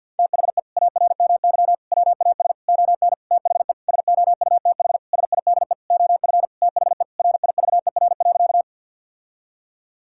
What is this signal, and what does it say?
Morse code, heard as a keyed tone, about 47 wpm, 700 Hz. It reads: THE ARMY WAS ON THE SOUTH SIDE OF THE RIVER.
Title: Sentences from Top 1000 Words - Review Part 5 - 50wpm